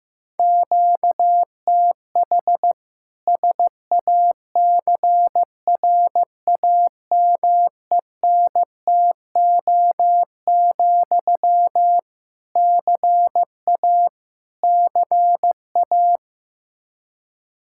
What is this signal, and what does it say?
Morse code, heard as a keyed tone, about 15 wpm, 700 Hz. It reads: QTH SACRAMENTO, CA CA
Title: K1USN SST QSO Practice 15wpm